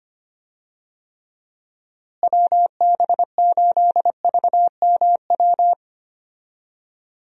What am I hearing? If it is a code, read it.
Morse code, heard as a keyed tone, about 25 wpm, 700 Hz. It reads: WB8VMW
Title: USA Call Signs - 6 Characters - Repeated 2x - 25wpm